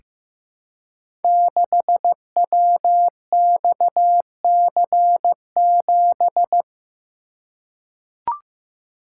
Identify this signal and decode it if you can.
Morse code, heard as a keyed tone, about 15 wpm, 700 Hz. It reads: 6WXC7
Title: Code Groups - 5 Letter:Numbers - 15wpm